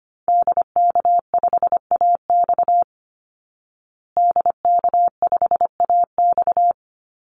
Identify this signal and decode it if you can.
Morse code, heard as a keyed tone, about 25 wpm, 700 Hz. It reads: DK5AX DK5AX
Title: Call Signs Repeated 2x - 25wpm